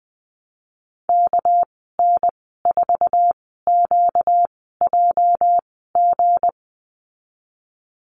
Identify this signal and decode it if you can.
Morse code, heard as a keyed tone, about 19 wpm, spaced as for 14 wpm, 700 Hz. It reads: KN4QJG
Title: 245 Lesson 39g - Letter z - Call Signs 20wpm